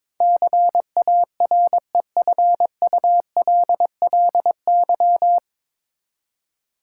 Morse code, heard as a keyed tone, about 23 wpm, 700 Hz. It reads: CAREFULLY
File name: 9 Letter Words - First 3 Letters Given - 22wpm